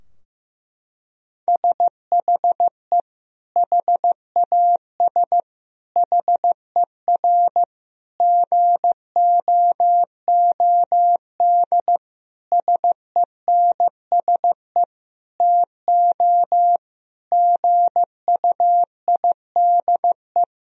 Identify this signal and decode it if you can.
Morse code, heard as a keyed tone, about 16 wpm, 700 Hz. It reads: SHE HAS HER GOOD SENSE TO GUIDE
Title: Sentences from Top 1000 Words - Review Part 4 - 15wpm